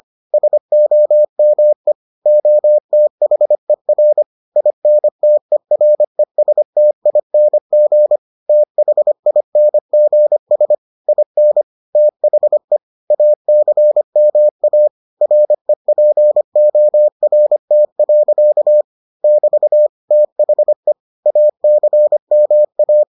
Morse code, heard as a keyed tone, about 25 wpm, 600 Hz. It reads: SOME OTHER INTERESTING THINGS IN THE ACMA REPORT. <BT> THE ACMA